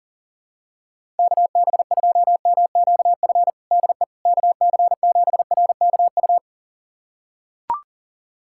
Morse code, heard as a keyed tone, about 40 wpm, 700 Hz. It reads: KB2MQF DE KC7RKU
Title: USA Callsign DE Callsign 40wpm